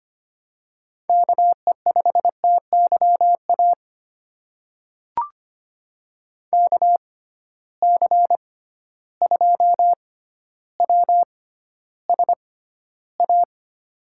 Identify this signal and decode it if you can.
Morse code, heard as a keyed tone, about 25 wpm, spaced as for 13 wpm, 700 Hz. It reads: KE5TYA KC2WSA
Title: USA Call Signs - 6x Character Spacing - 25wpm